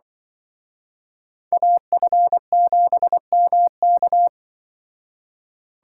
Morse code, heard as a keyed tone, about 24 wpm, 700 Hz. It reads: AF7MK
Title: Speed Racing - USA Call Signs 20wpm